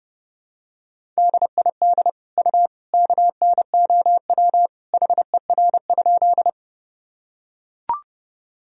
Morse code, heard as a keyed tone, about 30 wpm, 700 Hz. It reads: DID U KNOW HER?
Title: Sentences from Top 100 Words and Abbreviations 30wpm